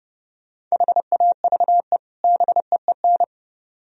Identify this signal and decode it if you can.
Morse code, heard as a keyed tone, about 39 wpm, 700 Hz. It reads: HAVE BEEN